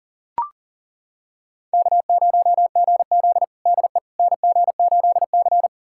Morse code, heard as a keyed tone, about 40 wpm, 700 Hz. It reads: K0GZ DE NG8C
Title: USA Callsign DE Callsign - 4 Characters - 40wpm